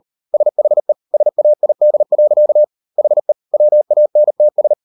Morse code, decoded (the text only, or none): SHE SAID. HE WANTS